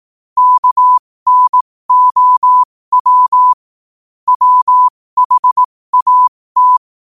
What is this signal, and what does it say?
Morse code, heard as a keyed tone, about 18 wpm, spaced as for 15 wpm, 1000 Hz. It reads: KNOW WHAT
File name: qotdfiles / 15wpm